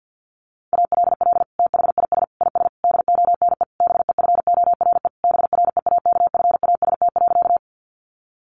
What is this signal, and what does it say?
Morse code, heard as a keyed tone, about 50 wpm, 700 Hz. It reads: ALL THIS IS DONE BEFORE BREAKFAST.
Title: Sentences from Top 1500 Words 50wpm